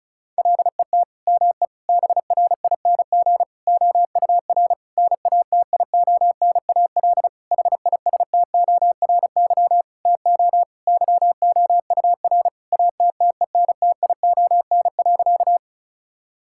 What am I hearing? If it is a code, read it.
Morse code, heard as a keyed tone, about 34 wpm, 700 Hz. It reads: LET ME BRING OUR NATIONAL HISTORY TO YOUR ATTENTION.
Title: Sentences from Top 1000 Words - Review Part 6 - 35wpm